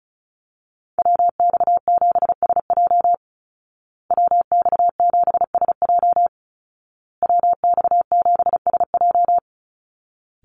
Morse code, heard as a keyed tone, about 34 wpm, 700 Hz. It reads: WX7SJ WX7SJ WX7SJ
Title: Call Signs Repeated 2x - 35wpm